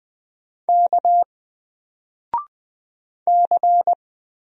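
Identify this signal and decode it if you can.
Morse code, heard as a keyed tone, about 20 wpm, 700 Hz. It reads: K C
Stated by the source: Single letter and number Morse drill at 20 wpm, from N4LCD.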